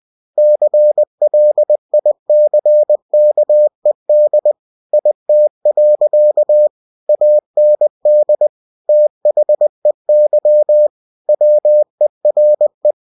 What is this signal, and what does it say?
Morse code, heard as a keyed tone, about 20 wpm, 600 Hz. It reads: CLICKED IT. AND THEY WERE
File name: cw-books / winnie-the-pooh / winnie-20wpm-page-0109